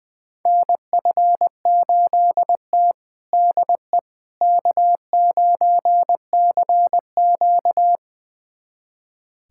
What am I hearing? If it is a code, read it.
Morse code, heard as a keyed tone, about 20 wpm, 700 Hz. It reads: NF8T DE K9CQ
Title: USA Callsign DE Callsign - 4 Characters - 20wpm